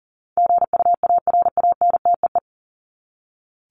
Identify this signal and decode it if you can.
Morse code, heard as a keyed tone, about 40 wpm, 700 Hz. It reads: GUARANTEE
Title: Top 1501-1750 Words 40wpm